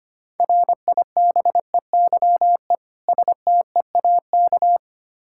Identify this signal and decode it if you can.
Morse code, heard as a keyed tone, about 25 wpm, 700 Hz. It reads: RIBEYE STEAK
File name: practice / 25wpm / Difficult Words with Hints - 25wpm